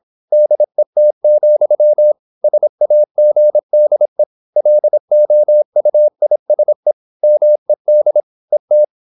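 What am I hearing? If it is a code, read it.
Morse code, heard as a keyed tone, about 26 wpm, 600 Hz. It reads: DET, SAGDE LOUISE MED ET
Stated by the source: High-speed (QRQ) Morse at 26 wpm.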